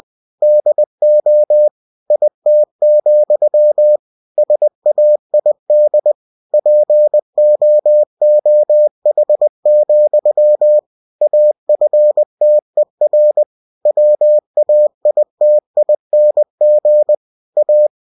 Morse code, heard as a keyed tone, about 20 wpm, 600 Hz. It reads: DO IT, SAID POOH, AFTER WAITING A
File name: cw-books / winnie-the-pooh / winnie-20wpm-page-0034